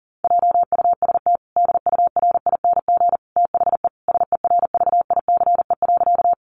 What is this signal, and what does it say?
Morse code, heard as a keyed tone, about 40 wpm, 700 Hz. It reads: JUST DURING THE SERVICE.
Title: Sentences from Top 700 Words 40wpm